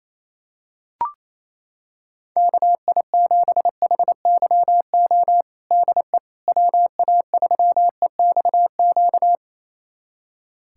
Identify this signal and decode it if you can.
Morse code, heard as a keyed tone, about 28 wpm, 700 Hz. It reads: KI7HYO DE WA3EXQ
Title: USA Callsign DE Callsign - 6 Characters - 28wpm